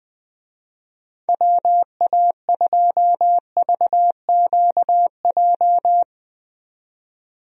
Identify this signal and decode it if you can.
Morse code, heard as a keyed tone, about 21 wpm, 700 Hz. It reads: WA2VQJ
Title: USA Call Signs - 6 Characters - 20wpm